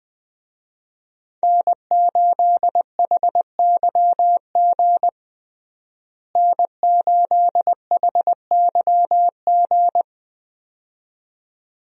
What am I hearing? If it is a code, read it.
Morse code, heard as a keyed tone, about 20 wpm, 700 Hz. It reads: N8HYG N8HYG
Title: USA Call Signs - 5 Characters - Repeated 2x - 20wpm